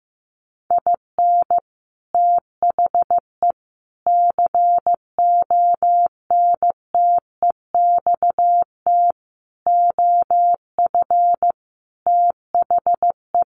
Morse code, heard as a keyed tone, about 15 wpm, 700 Hz. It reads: IN THE CONTEXT OF THE
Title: Sets of 5 Words 15wpm